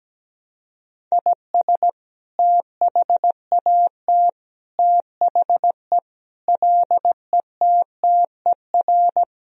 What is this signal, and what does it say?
Morse code, heard as a keyed tone, about 17 wpm, 700 Hz. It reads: IS THAT THE LETTER
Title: Sentences from Top 1000 Words - Review Part 2 - 17wpm